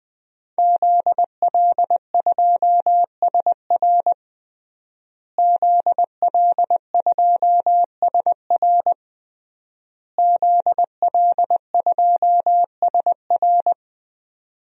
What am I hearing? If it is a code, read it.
Morse code, heard as a keyed tone, about 20 wpm, 700 Hz. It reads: ZL2SR ZL2SR ZL2SR
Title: DXCC-100 Call Signs - 5 Characters - Repeated 3x - 20wpm